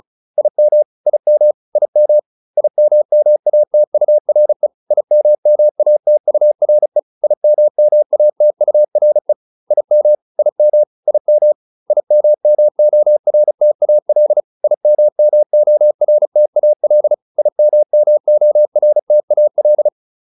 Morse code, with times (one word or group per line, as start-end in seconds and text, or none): 0.4-0.8 IM
1.1-1.5 IM
1.8-2.2 IM
2.6-4.7 IMMATURE
4.9-7.0 IMMATURE
7.2-9.3 IMMATURE
9.7-10.2 IM
10.4-10.8 IM
11.1-11.5 IM
11.9-14.4 IMMORTAL
14.6-17.1 IMMORTAL
17.4-19.9 IMMORTAL